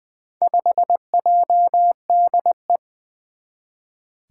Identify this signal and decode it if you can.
Morse code, heard as a keyed tone, about 20 wpm, 700 Hz. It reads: 5JDE
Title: Four Challenging Letter-Number Combinations 20wpm